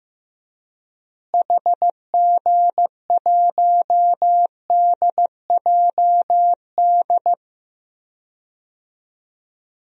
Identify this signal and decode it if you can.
Morse code, heard as a keyed tone, about 15 wpm, 700 Hz. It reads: HG1DJD
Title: DXCC-100 Call Signs - 6 Characters - Repeated 3x - 15wpm